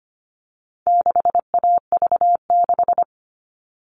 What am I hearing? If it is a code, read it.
Morse code, heard as a keyed tone, about 25 wpm, 700 Hz. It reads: 6AV6
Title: All Cut Numbers - 4-Digit 25wpm